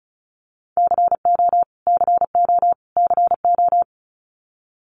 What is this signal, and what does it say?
Morse code, heard as a keyed tone, about 34 wpm, 700 Hz. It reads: CO CO CO
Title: US States Repeated 3x 35wpm